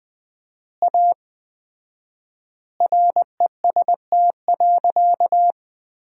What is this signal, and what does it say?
Morse code, heard as a keyed tone, about 20 wpm, 700 Hz. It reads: A REST.
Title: 4x Word-Spacing - Sentences from Top 500 Words - 20wpm